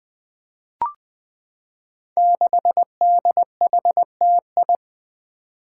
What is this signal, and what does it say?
Morse code, heard as a keyed tone, about 20 wpm, 700 Hz. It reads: 6DHTI
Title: Five Challenging Letter-Number Combinations 20wpm